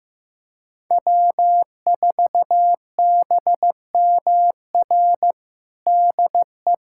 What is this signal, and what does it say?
Morse code, heard as a keyed tone, about 15 wpm, 700 Hz. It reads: W4BMR DE
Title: USA Callsign DE Callsign - 5 Characters - 15wpm